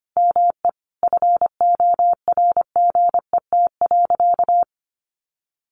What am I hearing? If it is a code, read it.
Morse code, heard as a keyed tone, about 25 wpm, 700 Hz. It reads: ME FORGET.
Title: Sentences from Top 1500 Words 25wpm